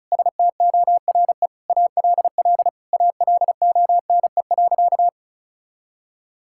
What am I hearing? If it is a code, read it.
Morse code, heard as a keyed tone, about 36 wpm, 700 Hz. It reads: STORE ALL ALONE.
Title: Sentences from Top 1000 Words - Review Part 5 - 35wpm